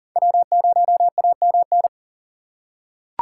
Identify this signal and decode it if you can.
Morse code, heard as a keyed tone, about 40 wpm, 700 Hz. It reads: W0AMN
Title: USA Call Signs - 5 Characters - 2x Character Spacing - 40wpm